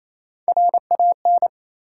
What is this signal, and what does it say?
Morse code, heard as a keyed tone, about 28 wpm, 700 Hz. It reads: RAN
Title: Sentences from Top 500 Words - New Word Drill - 28wpm